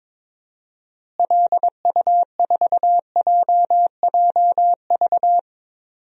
Morse code, heard as a keyed tone, about 22 wpm, 700 Hz. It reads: LU4JJV
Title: DXCC-100 Call Signs - 6 Characters - Repeated 3x - 22wpm